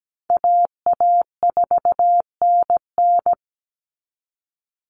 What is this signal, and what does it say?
Morse code, heard as a keyed tone, about 17 wpm, 700 Hz. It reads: AA4NN
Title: Call Signs Repeated 3x - 17wpm